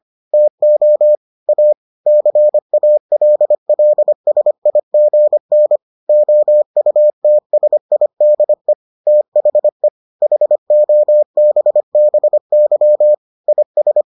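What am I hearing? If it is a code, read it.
Morse code, heard as a keyed tone, about 25 wpm, 600 Hz. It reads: TO A CALLSIGN OUTSIDE THE HOBBY IS
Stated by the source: Morse edition of the Foundations of Amateur Radio episodes.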